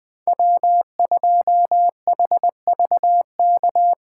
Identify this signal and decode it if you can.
Morse code, heard as a keyed tone, about 20 wpm, 700 Hz. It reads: W2HVK
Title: Speed Racing - USA 1x3 Call Signs 17wpm